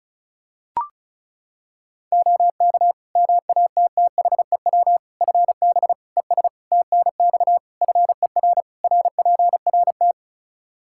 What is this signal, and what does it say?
Morse code, heard as a keyed tone, about 34 wpm, 700 Hz. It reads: OK MATTHEW FB ES TNX FER RPRT
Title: Practice QSO - Exchange 2 - Info - 35wpm